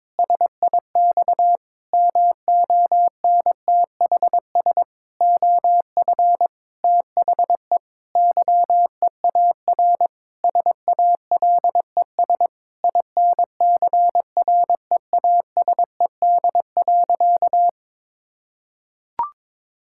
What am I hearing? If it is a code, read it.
Morse code, heard as a keyed tone, about 22 wpm, 700 Hz. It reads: SIX MONTHS OF THE YEAR SALES INCREASED.
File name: practice / 22wpm / Sentences with 8-Word Phrases - 22wpm